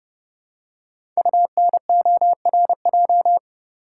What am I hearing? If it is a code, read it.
Morse code, heard as a keyed tone, about 30 wpm, 700 Hz. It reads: UNORJ
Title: Code Groups - 5 Letters - 30wpm